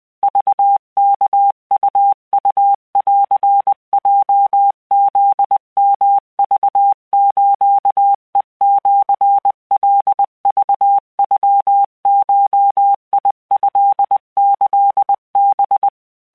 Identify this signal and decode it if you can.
Morse code (Japanese Wabun code), heard as a keyed tone, about 20 wpm, 800 Hz. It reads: クワウウンヲフヨクスヘシカクノゴトキハ